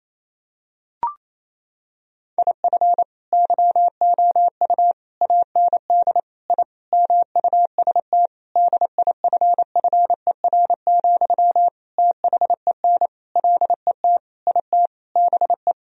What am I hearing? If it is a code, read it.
Morse code, heard as a keyed tone, about 28 wpm, 700 Hz. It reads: IF YOU AND I MUST DIFFER, THEN LET IT BE
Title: Sentences from Top 1000 Words - Review Part 1 - 28wpm